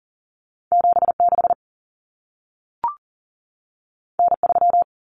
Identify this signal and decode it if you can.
Morse code, heard as a keyed tone, about 40 wpm, 700 Hz. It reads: 76 N3
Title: Common Cut Numbers - 2-Digit 40wpm